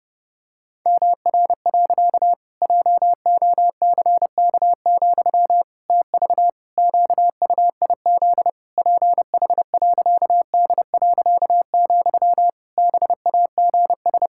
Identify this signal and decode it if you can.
Morse code, heard as a keyed tone, about 30 wpm, 700 Hz. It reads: MR. JOCK, TV QUIZ PH.D., BAGS
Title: Pangram Challenge v2 - 30wpm